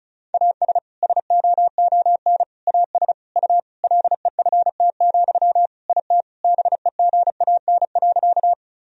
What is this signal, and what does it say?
Morse code, heard as a keyed tone, about 36 wpm, 700 Hz. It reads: AS SOON AS U LEFT, IT BEGAN.
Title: Sentences from Top 400 Words and Abbreviations 35wpm